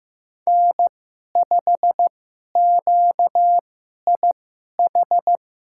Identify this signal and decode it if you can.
Morse code, heard as a keyed tone, about 15 wpm, spaced as for 11 wpm, 700 Hz. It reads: N5QIH
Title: USA Call Signs - 5 Characters - 2x Character Spacing - 15wpm